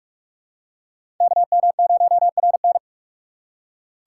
Morse code, heard as a keyed tone, about 42 wpm, 700 Hz. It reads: KM0RN